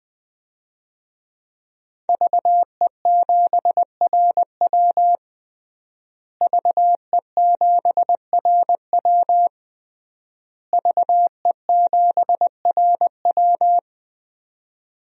Morse code, heard as a keyed tone, about 20 wpm, 700 Hz. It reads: VE7RW VE7RW VE7RW
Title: DXCC-100 Call Signs - 5 Characters - Repeated 3x - 20wpm